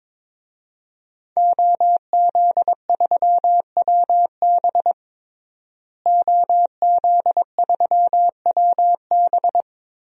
Morse code, heard as a keyed tone, about 22 wpm, 700 Hz. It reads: OZ3WB OZ3WB
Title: DXCC-100 Call Signs - 5 Characters - Repeated 3x - 22wpm